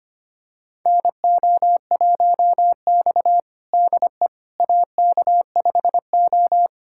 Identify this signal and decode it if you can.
Morse code, heard as a keyed tone, about 25 wpm, 700 Hz. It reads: NO1X DE AK5O
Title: USA Callsign DE Callsign - 4 Characters - 25wpm